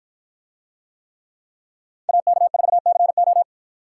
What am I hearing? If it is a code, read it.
Morse code, heard as a keyed tone, about 53 wpm, 700 Hz. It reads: AK4CO